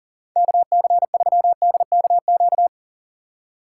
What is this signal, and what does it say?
Morse code, heard as a keyed tone, about 49 wpm, 700 Hz. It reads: KC3DKQ
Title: Speed Racing - USA 2x3 Call Signs 40wpm